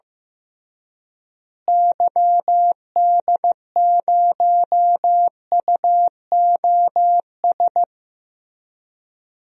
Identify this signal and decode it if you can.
Morse code, heard as a keyed tone, about 15 wpm, 700 Hz. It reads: YD0UOS
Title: DXCC-100 Call Signs - 6 Characters - Repeated 2x - 15wpm